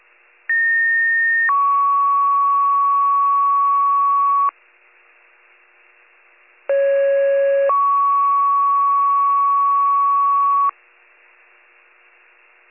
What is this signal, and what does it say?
Two-tone page, heard as a one-second tone, then a three-second tone, about 1850 then 1120 Hz, about 550 then 1120 Hz. It is heard from 0.5 to 10.7 s.